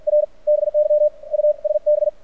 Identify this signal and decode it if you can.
Morse code, heard as a keyed tone, about 56 wpm, spaced as for 47 wpm, 600 Hz. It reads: R BY USB
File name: archived / CRSnet